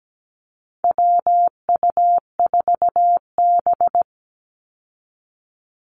Morse code, heard as a keyed tone, about 17 wpm, 700 Hz. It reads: WU4B
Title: Call Signs - 17wpm